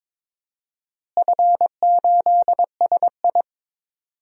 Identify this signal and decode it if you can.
Morse code, heard as a keyed tone, about 22 wpm, 700 Hz. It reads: F8SI
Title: DXCC-100 Call Signs - 4 Characters - Repeated 2x - 22wpm